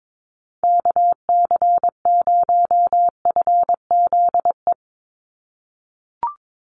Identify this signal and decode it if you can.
Morse code, heard as a keyed tone, about 22 wpm, 700 Hz. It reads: KC0FZE
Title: USA Call Signs 22wpm